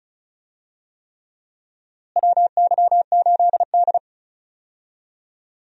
Morse code, heard as a keyed tone, about 36 wpm, 700 Hz. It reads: WY8D
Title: USA Call Signs - 4 Characters - Repeated 3x - 35wpm